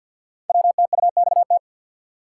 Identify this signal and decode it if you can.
Morse code, heard as a keyed tone, about 52 wpm, 700 Hz. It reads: WTUXT
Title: Code Groups - 5 Letters - 50wpm